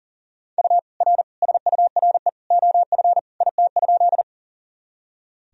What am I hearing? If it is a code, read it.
Morse code, heard as a keyed tone, about 40 wpm, 700 Hz. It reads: U R SURE OF IT?
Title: Sentences from Top 300 Words and Abbreviations 40wpm